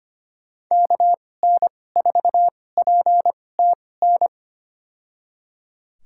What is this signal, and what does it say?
Morse code, heard as a keyed tone, about 25 wpm, spaced as for 18 wpm, 700 Hz. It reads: KN4PTN
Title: USA Call Signs - 6 Characters - 2x Character Spacing - 25wpm